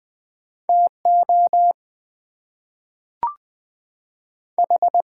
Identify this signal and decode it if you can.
Morse code, heard as a keyed tone, about 20 wpm, 700 Hz. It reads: TO H
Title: Two Letter Words - 20wpm